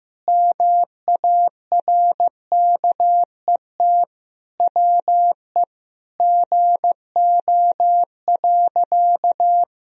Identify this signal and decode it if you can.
Morse code, heard as a keyed tone, about 16 wpm, 700 Hz. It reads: MARKET WE GO.
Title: Sentences from Top 1000 Words - Review Part 3 - 15wpm